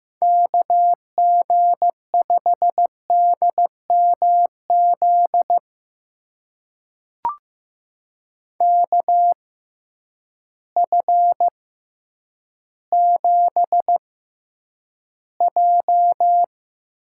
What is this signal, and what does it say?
Morse code, heard as a keyed tone, about 15 wpm, spaced as for 9 wpm, 700 Hz. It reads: KG5DMZ KF7J
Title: USA Call Signs - 6 Characters - 6x Character Spacing - 15wpm